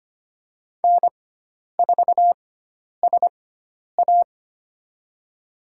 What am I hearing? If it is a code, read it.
Morse code, heard as a keyed tone, about 25 wpm, spaced as for 10 wpm, 700 Hz. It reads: N4SA
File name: Lesson 16g - Letter c - Call Signs 25wpm